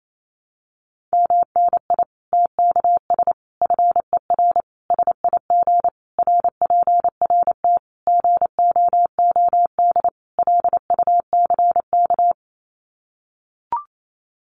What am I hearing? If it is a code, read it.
Morse code, heard as a keyed tone, about 28 wpm, 700 Hz. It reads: MNI TKS FER SIG RPRT GOOD LUCK
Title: Common QSO Phrases 28wpm